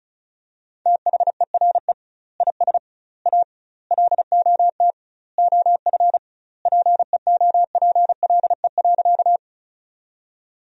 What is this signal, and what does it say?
Morse code, heard as a keyed tone, about 36 wpm, 700 Hz. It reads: THERE IS A LOT OF PEOPLE.
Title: 2x Word-Spacing - Sentences from Top 500 Words - 35wpm